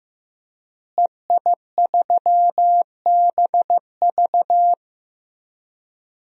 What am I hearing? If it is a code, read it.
Morse code, heard as a keyed tone, about 15 wpm, 700 Hz. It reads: EI3BV
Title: DXCC-100 Call Signs - 5 Characters - Repeated 3x - 15wpm